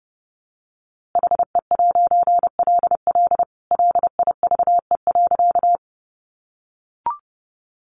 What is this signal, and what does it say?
Morse code, heard as a keyed tone, about 30 wpm, 700 Hz. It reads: HE'LL LIVE.
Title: A Sound Of Thunder 30wpm